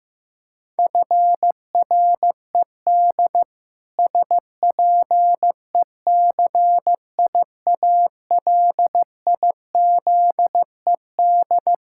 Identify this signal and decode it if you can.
Morse code, heard as a keyed tone, about 15 wpm, 700 Hz. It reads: FRED SPECIALIZED
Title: Pangram Challenge v2 - 15wpm